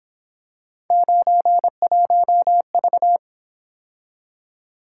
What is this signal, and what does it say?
Morse code, heard as a keyed tone, about 26 wpm, 700 Hz. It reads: 91V
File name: Speed-Racing - Three Letter-Number Combinations - 22wpm